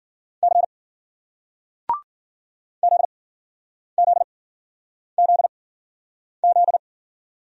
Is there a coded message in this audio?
Yes: Morse code from 0.4 to 6.8 s, heard as a keyed tone, about 50 wpm, spaced as for 9 wpm, 700 Hz. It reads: C ZZZZ